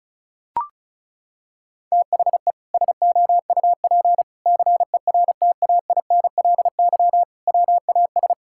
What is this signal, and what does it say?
Morse code, heard as a keyed tone, about 34 wpm, 700 Hz. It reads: THE SOUP CERTAINLY WAS
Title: Sentences from Top 1000 Words - Review Part 6 - 35wpm